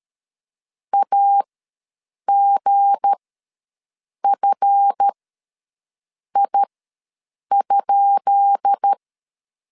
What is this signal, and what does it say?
Morse code, heard as a keyed tone, about 13 wpm, spaced as for 6 wpm, 800 Hz. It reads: AGFI?
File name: LEZIONI / 10 SIGNS / 10B INT ws 12wpm 3527